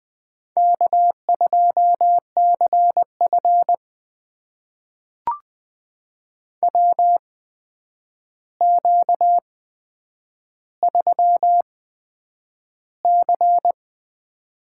Morse code, heard as a keyed tone, about 20 wpm, spaced as for 9 wpm, 700 Hz. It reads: K2CF WQ3C
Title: USA Call Signs - 4 Characters - 8x Character Spacing - 20wpm